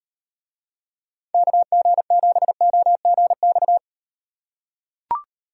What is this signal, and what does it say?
Morse code, heard as a keyed tone, about 38 wpm, 700 Hz. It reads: KG7OGX